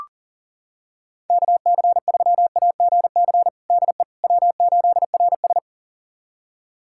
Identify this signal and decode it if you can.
Morse code, heard as a keyed tone, about 40 wpm, 700 Hz. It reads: KC3AGC DE W8RS
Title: USA Callsign DE Callsign 40wpm